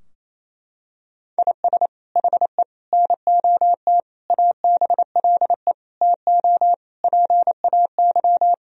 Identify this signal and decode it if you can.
Morse code, heard as a keyed tone, about 27 wpm, 700 Hz. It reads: IS HE NOT ABLE TO PAY
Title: Sentences from Top 1000 Words - Review Part 2 - 28wpm